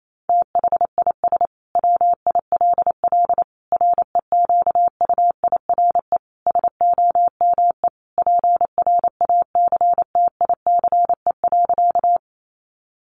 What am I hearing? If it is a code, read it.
Morse code, heard as a keyed tone, about 28 wpm, 700 Hz. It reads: THIS WILL REQUIRE SOME PRACTICE.
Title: Sentences from Top 900 Words 28wpm